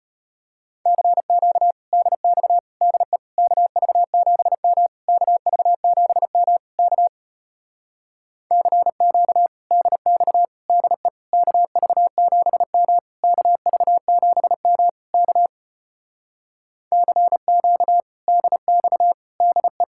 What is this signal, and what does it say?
Morse code, heard as a keyed tone, about 33 wpm, 700 Hz. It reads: CQ DX DE KV7M KV7M K CQ DX DE KV7M KV7M K CQ DX DE